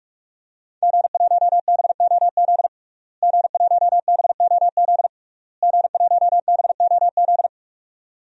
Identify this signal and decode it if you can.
Morse code, heard as a keyed tone, about 46 wpm, 700 Hz. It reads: G1BOZ G1BOZ G1BOZ
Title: DXCC-100 Call Signs - 5 Characters - Repeated 3x - 45wpm